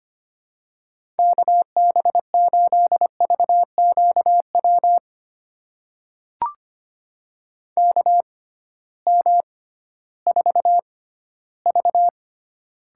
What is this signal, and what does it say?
Morse code, heard as a keyed tone, about 25 wpm, spaced as for 14 wpm, 700 Hz. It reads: KB8VQW KM4V
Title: USA Call Signs - 6 Characters - 6x Character Spacing - 25wpm